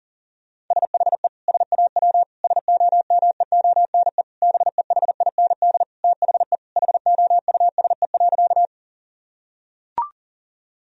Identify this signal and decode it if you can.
Morse code, heard as a keyed tone, about 37 wpm, 690 Hz. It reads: SHE SAW SOMEONE BEHIND THE HOUSE.
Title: Sentences from Top 1000 Words - Review Part 4 - 40wpm